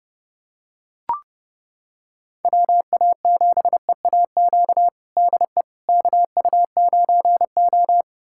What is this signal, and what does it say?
Morse code, heard as a keyed tone, about 30 wpm, 700 Hz. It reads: WA7EAQ DE KU9O
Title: USA Callsign DE Callsign 30wpm